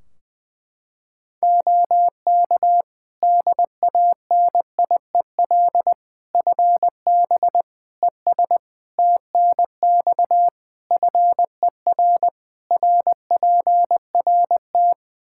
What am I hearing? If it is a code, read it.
Morse code, heard as a keyed tone, about 20 wpm, 700 Hz. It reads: OK DANIEL FB ES TNX FER RPRT
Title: K1USN SST QSO Practice 20wpm